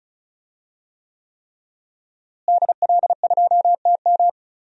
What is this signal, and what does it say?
Morse code, heard as a keyed tone, about 36 wpm, 700 Hz. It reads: DL2TM
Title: DXCC-100 Call Signs - 5 Characters - Repeated 3x - 35wpm